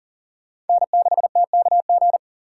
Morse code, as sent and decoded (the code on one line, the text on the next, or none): -. -.... - -.- --.
N6TKG